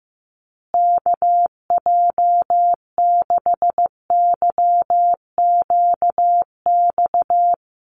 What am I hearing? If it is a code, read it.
Morse code, heard as a keyed tone, about 15 wpm, 700 Hz. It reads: KJ6YQX
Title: USA Call Signs Repeated 3x 15wpm